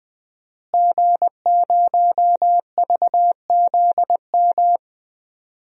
Morse code, heard as a keyed tone, about 21 wpm, 700 Hz. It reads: G0VZM
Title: DXCC-100 Call Signs - 5 Characters - Repeated 3x - 20wpm